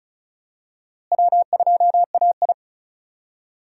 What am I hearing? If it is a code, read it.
Morse code, heard as a keyed tone, about 36 wpm, 700 Hz. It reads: W2AI
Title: USA Call Signs - 4 Characters - Repeated 3x - 35wpm